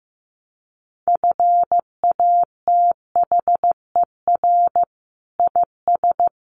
Morse code, heard as a keyed tone, about 15 wpm, 700 Hz. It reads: FATHER IS
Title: Sentences from Top 1000 Words 15wpm